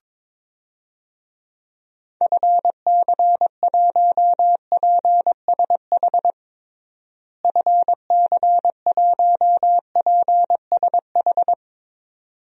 Morse code, heard as a keyed tone, about 22 wpm, 700 Hz. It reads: FC1PSH FC1PSH
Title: DXCC-100 Call Signs - 6 Characters - Repeated 2x - 22wpm